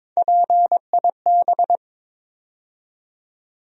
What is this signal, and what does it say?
Morse code, heard as a keyed tone, about 22 wpm, 700 Hz. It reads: PIB
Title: USA Call Sign Postfixes - 3 Letters 22wpm